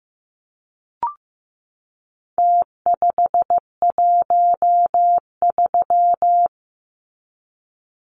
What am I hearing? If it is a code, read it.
Morse code, heard as a keyed tone, about 15 wpm, 700 Hz. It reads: T513
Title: Numbers - Contest Style - 4-Digit 15wpm